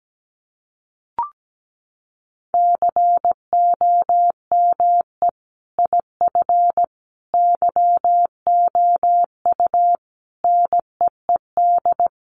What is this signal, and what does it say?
Morse code, heard as a keyed tone, about 17 wpm, 700 Hz. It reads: COME IF YOU NEED
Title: Sentences from Top 1000 Words 17wpm